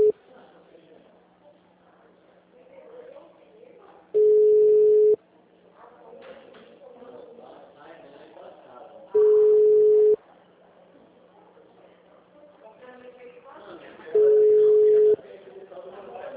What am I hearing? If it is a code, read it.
Ringback tone (Europe, 425 Hz), with speech.
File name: out- -2002- - - .572